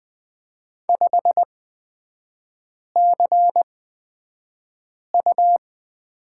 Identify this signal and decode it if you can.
Morse code, heard as a keyed tone, about 20 wpm, spaced as for 5 wpm, 700 Hz. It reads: 5CU
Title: 224 Lesson 36e - BT - Three Characters 20wpm